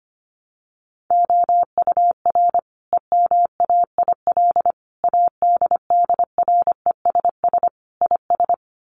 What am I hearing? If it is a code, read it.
Morse code, heard as a keyed tone, about 25 wpm, 700 Hz. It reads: OUR EMAIL ADDRESS IS
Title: Sets of 5 Words Encore2 25wpm